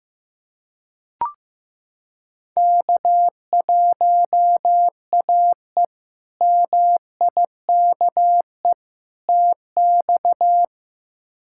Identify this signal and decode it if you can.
Morse code, heard as a keyed tone, about 15 wpm, 700 Hz. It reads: K1AE MIKE TX
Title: K1USN SST QSO Practice - 15wpm